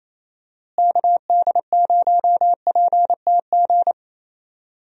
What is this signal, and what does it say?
Morse code, heard as a keyed tone, about 28 wpm, 700 Hz. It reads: KD0PTG
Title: USA Call Signs - 6 Characters - Repeated 3x - 28wpm